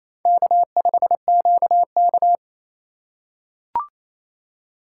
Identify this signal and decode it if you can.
Morse code, heard as a keyed tone, about 28 wpm, 700 Hz. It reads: K5QK
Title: USA Call Signs - 2x Character Spacing - 28wpm